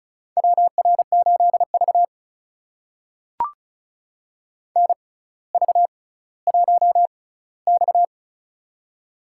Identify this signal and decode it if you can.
Morse code, heard as a keyed tone, about 36 wpm, spaced as for 18 wpm, 700 Hz. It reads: WR8V NV1X
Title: USA Call Signs - 4 Characters - 6x Character Spacing - 35wpm